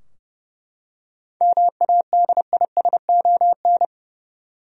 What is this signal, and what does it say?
Morse code, heard as a keyed tone, about 30 wpm, 700 Hz. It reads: MADISON
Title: USA Common Names - 30wpm